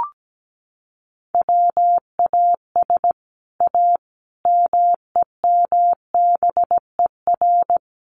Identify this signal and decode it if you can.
Morse code, heard as a keyed tone, about 17 wpm, 700 Hz. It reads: WAS A MEMBER
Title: Sets of 5 Words 17wpm